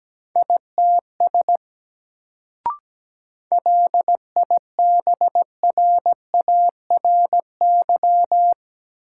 Morse code, heard as a keyed tone, about 17 wpm, 700 Hz. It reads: ITS LIBRARY
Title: Sentences from Top 600 Words - New Word Drill - 17wpm